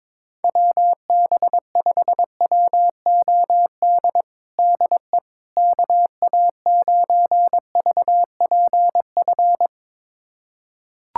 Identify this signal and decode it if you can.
Morse code, heard as a keyed tone, about 21 wpm, 700 Hz. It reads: WB5WOD DE KA9VPF
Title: USA Callsign DE Callsign - 6 Characters - 22wpm